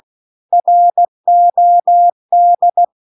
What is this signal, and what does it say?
Morse code, heard as a keyed tone, about 16 wpm, 700 Hz. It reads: ROD